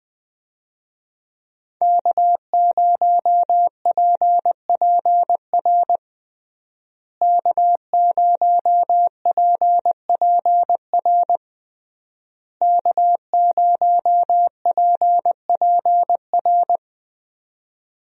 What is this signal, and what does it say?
Morse code, heard as a keyed tone, about 21 wpm, 700 Hz. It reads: K0PPR K0PPR K0PPR
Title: USA Call Signs - 5 Characters - Repeated 3x - 20wpm